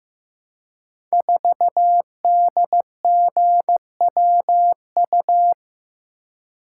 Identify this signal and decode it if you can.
Morse code, heard as a keyed tone, about 15 wpm, 700 Hz. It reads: 4DGWU
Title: Code Groups - 5 Letter:Numbers - 15wpm